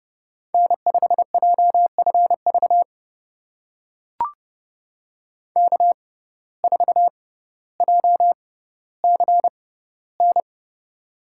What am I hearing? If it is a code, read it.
Morse code, heard as a keyed tone, about 32 wpm, spaced as for 16 wpm, 700 Hz. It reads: N5JFV K4JCN
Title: USA Call Signs - 5 Characters - 6x Character Spacing - 30wpm